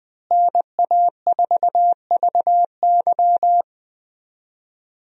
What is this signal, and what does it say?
Morse code, heard as a keyed tone, about 20 wpm, 700 Hz. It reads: NA4VY
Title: USA Call Signs - 5 Characters - Repeated 2x - 20wpm